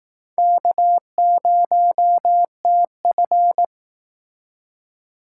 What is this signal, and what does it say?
Morse code, heard as a keyed tone, about 19 wpm, 700 Hz. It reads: K0TF